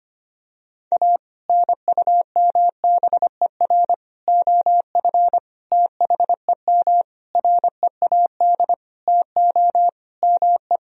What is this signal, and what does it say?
Morse code, heard as a keyed tone, about 25 wpm, 700 Hz. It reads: A NUMBER OF THEM READ TO ME
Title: Sentences from Top 1000 Words - Review Part 1 - 25wpm